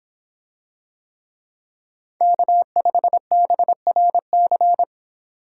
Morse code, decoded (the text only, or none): K5BRC